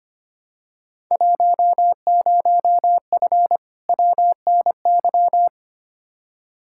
Morse code, heard as a keyed tone, about 25 wpm, 700 Hz. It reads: 10F WNY
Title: Field Day Exchanges Encore - 25wpm